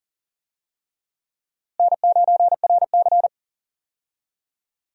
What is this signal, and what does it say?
Morse code, heard as a keyed tone, about 40 wpm, 700 Hz. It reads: N9RC
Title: USA Call Signs - 4 Characters - Repeated 3x - 40wpm